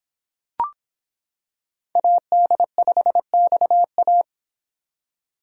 Morse code, heard as a keyed tone, about 25 wpm, 700 Hz. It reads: AD5XA